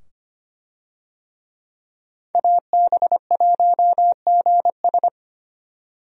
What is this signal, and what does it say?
Morse code, heard as a keyed tone, about 25 wpm, 690 Hz. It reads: AB1GS
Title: USA Call Signs - 5 Characters - 25wpm